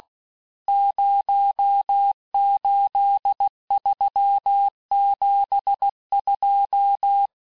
Morse code, heard as a keyed tone, about 16 wpm, 800 Hz. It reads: 08372
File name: cours-du-lundi files / Cours-1586-texte-chiffre-vitesse-15mpm